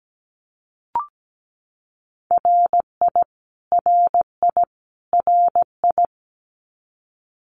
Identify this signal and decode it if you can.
Morse code, heard as a keyed tone, about 17 wpm, 700 Hz. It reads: RI RI RI
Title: US States Repeated 3x 17wpm